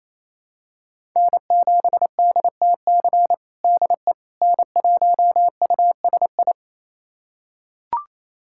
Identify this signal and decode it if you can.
Morse code, heard as a keyed tone, about 28 wpm, 700 Hz. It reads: N7DTC DE N1USI
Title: USA Callsign DE Callsign - 5 Characters - 28wpm